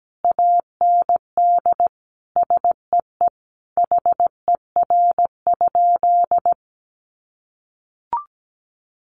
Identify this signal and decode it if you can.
Morse code, heard as a keyed tone, about 17 wpm, 700 Hz. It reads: AND SEE HER?
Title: Sentences from Top 100 Words 17wpm